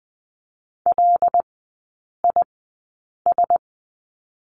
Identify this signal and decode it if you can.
Morse code, heard as a keyed tone, about 20 wpm, spaced as for 8 wpm, 690 Hz. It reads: LIS
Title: 081 Lesson 14e - Number 2 - Three Characters 20wpm